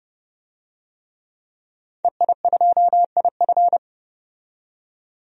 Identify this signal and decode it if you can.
Morse code, heard as a keyed tone, about 31 wpm, 700 Hz. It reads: EI2IF
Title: DXCC-100 Call Signs - 5 Characters - 30wpm